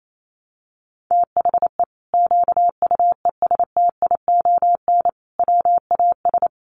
Morse code, heard as a keyed tone, about 28 wpm, 700 Hz. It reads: THE QUESTION WAS
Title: Sentences from Top 1200 Words 28wpm